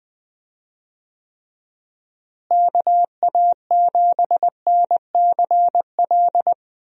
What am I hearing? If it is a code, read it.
Morse code, heard as a keyed tone, about 19 wpm, 700 Hz. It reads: KA7NCL
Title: USA Call Signs - 6 Characters - Repeated 3x - 20wpm